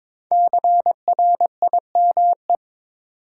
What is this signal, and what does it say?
Morse code, heard as a keyed tone, about 22 wpm, 700 Hz. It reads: CRIME
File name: 5 Letter Words - First 3 Letters Given - 22wpm